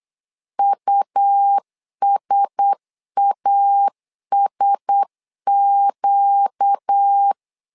Morse code, heard as a keyed tone, about 8 wpm, 800 Hz. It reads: USASQ
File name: LEZIONI / 4 Q1S 8wpm 2710